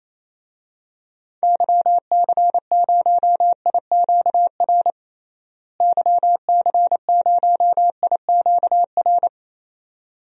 Morse code, heard as a keyed tone, about 28 wpm, 700 Hz. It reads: YC0IQR YC0IQR
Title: DXCC-100 Call Signs - 6 Characters - Repeated 2x - 28wpm